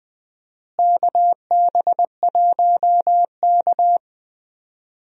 Morse code, heard as a keyed tone, about 20 wpm, 700 Hz. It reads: KB1K
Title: USA Call Signs - 4 Characters - Repeated 2x - 20wpm